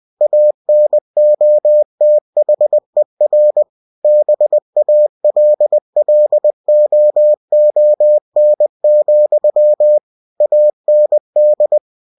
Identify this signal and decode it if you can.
Morse code, heard as a keyed tone, about 20 wpm, 600 Hz. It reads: ANOTHER BALLOON, AND
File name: cw-books / winnie-the-pooh / winnie-20wpm-page-0053